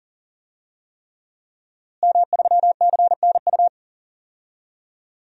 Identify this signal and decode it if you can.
Morse code, heard as a keyed tone, about 40 wpm, 700 Hz. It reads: M3CNU
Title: DXCC-100 Call Signs - 5 Characters - Repeated 3x - 40wpm